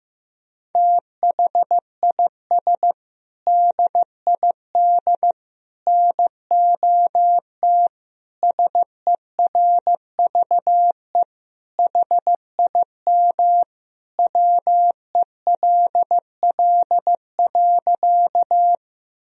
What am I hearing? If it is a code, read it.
Morse code, heard as a keyed tone, about 16 wpm, 700 Hz. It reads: THIS DID NOT SERVE HIM WELL.
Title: Sentences from Top 1000 Words - Review Part 2 - 15wpm